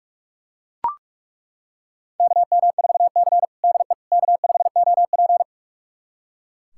Morse code, heard as a keyed tone, about 41 wpm, 700 Hz. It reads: KM4C DE K5OP
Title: USA Callsign DE Callsign - 4 Characters - 45wpm